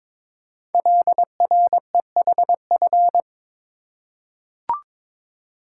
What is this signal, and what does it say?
Morse code, heard as a keyed tone, about 22 wpm, 700 Hz. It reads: LREHF
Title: Code Groups - 5 Letters - 22wpm